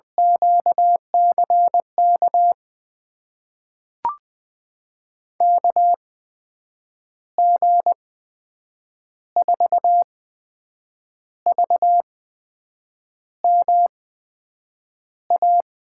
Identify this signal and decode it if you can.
Morse code, heard as a keyed tone, about 20 wpm, spaced as for 7 wpm, 700 Hz. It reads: QCK KG4VMA
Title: USA Call Signs - 6 Characters - 8x Character Spacing - 20wpm